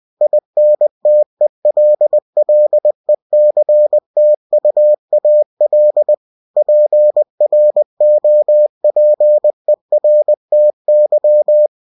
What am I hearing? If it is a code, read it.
Morse code, heard as a keyed tone, about 20 wpm, 600 Hz. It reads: INTELLECTUAL PROPERTY
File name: cw-books / winnie-the-pooh / winnie-20wpm-page-0113